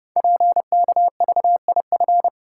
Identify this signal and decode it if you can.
Morse code, heard as a keyed tone, about 29 wpm, 700 Hz. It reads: PKVIF
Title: Code Groups - 5 Letters - 30wpm